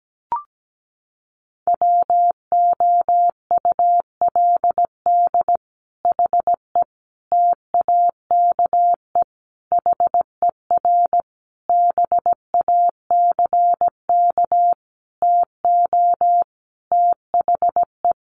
Morse code, heard as a keyed tone, about 17 wpm, 700 Hz. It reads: WOULD HE TAKE HER BACK TO THE
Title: Sentences from Top 1500 Words 17wpm